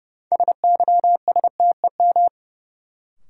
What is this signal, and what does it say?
Morse code, heard as a keyed tone, about 29 wpm, 700 Hz. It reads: SYSTEM
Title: Sentences from Top 1000 Words - Review Part 3 - New Word Drill - 30wpm